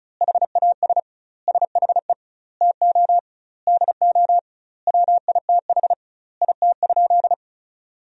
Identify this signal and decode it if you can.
Morse code, heard as a keyed tone, about 36 wpm, 700 Hz. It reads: HAS SHE TO DO WITH IT?
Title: 2x Word-Spacing - Sentences from Top 100 Words - 35wpm